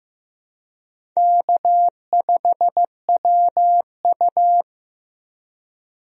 Morse code, heard as a keyed tone, about 16 wpm, 700 Hz. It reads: K5WU
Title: USA Call Signs - 4 Characters - Repeated 3x - 15wpm